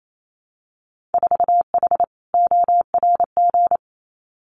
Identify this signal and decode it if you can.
Morse code, heard as a keyed tone, about 28 wpm, 700 Hz. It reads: 4H ORG
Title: Winter Field Day Exchanges Encore 28wpm